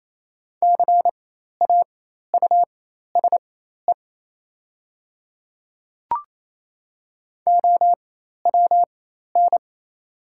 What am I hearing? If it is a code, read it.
Morse code, heard as a keyed tone, about 28 wpm, spaced as for 13 wpm, 700 Hz. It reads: CAUSE OWN
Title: 4x Character Spacing - Top 300 Words - 28wpm